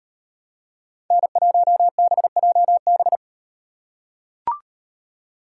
Morse code, heard as a keyed tone, about 36 wpm, 700 Hz. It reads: N1BJB